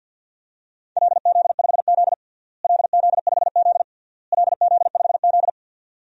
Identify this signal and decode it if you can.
Morse code, heard as a keyed tone, about 53 wpm, 700 Hz. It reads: LZ5Z LZ5Z LZ5Z
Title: DXCC-100 Call Signs - 4 Characters - Repeated 3x - 50wpm